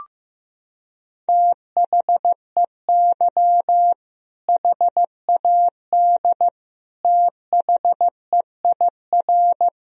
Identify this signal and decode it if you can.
Morse code, heard as a keyed tone, about 15 wpm, 700 Hz. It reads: THEY HAD THEIR
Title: Sentences from Top 1000 Words - Review Part 3 - 15wpm